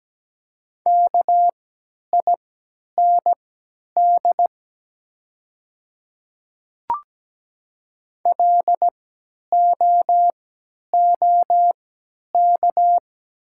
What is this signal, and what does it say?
Morse code, heard as a keyed tone, about 17 wpm, spaced as for 10 wpm, 690 Hz. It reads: KIND LOOK
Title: 3x Character Spacing - Top 200 Words - 17wpm